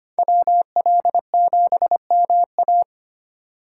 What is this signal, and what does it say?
Morse code, heard as a keyed tone, about 25 wpm, 700 Hz. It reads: WL7MA
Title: USA Call Signs - 3x Character Spacing - 25wpm